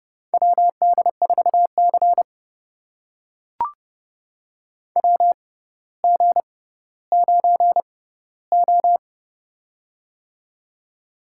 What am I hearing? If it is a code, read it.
Morse code, heard as a keyed tone, about 32 wpm, spaced as for 16 wpm, 700 Hz. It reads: WD4C WG9O
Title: USA Call Signs - 4 Characters - 6x Character Spacing - 30wpm